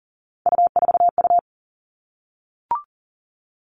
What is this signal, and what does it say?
Morse code, heard as a keyed tone, about 40 wpm, 700 Hz. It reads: U4U
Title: Three Challenging Letter-Number Combinations 40wpm